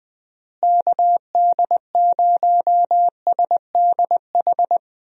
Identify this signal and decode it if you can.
Morse code, heard as a keyed tone, about 21 wpm, 700 Hz. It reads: KD0SDH
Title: USA Call Signs - 6 Characters - Repeated 3x - 20wpm